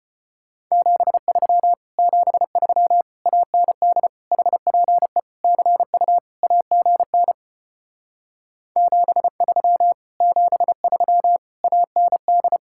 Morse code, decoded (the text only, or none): 73 73 AND HPE CU AGN 73 73 AND